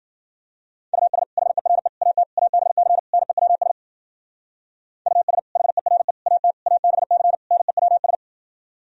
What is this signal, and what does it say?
Morse code, heard as a keyed tone, about 51 wpm, 700 Hz. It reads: US HERE AT ABC NEWS US HERE AT ABC NEWS